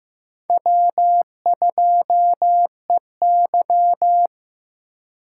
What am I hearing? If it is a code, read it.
Morse code, heard as a keyed tone, about 18 wpm, 700 Hz. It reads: W2EY
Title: Speed Racing - USA Call Signs 15wpm